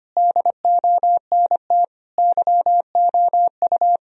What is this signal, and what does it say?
Morse code, heard as a keyed tone, about 25 wpm, 700 Hz. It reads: DONT YOU
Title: Sentences from Top 1000 Words - Review Part 4 - 25wpm